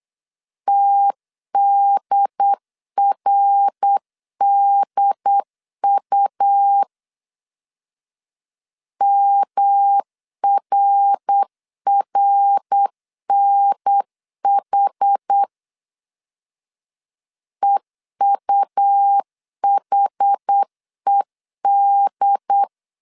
Morse code, heard as a keyed tone, about 8 wpm, 800 Hz. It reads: TDRDU MRRNH EUHED